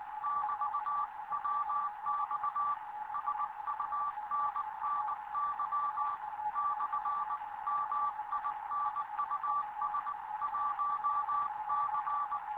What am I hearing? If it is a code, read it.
Morse code, heard as a keyed tone, about 19 wpm, 1160 Hz. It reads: <BT> WX SUNNY / MINUS 1C
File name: Morze-HA2MN / hangp / 2 11 GYAKORLAT